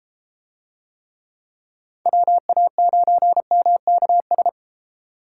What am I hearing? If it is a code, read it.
Morse code, heard as a keyed tone, about 33 wpm, 700 Hz. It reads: WA9MKS